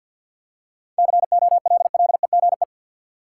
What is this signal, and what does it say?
Morse code, heard as a keyed tone, about 53 wpm, 700 Hz. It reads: COLLEGE
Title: Seven Letter Words - 50wpm